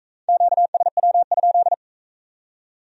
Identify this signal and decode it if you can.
Morse code, heard as a keyed tone, about 38 wpm, 700 Hz. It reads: QSW?